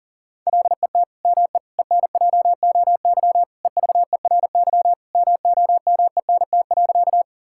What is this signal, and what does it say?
Morse code, heard as a keyed tone, about 40 wpm, 700 Hz. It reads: LET ME ENJOY EVERY MOMENT.
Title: Sentences from Top 1000 Words - Review Part 5 - 40wpm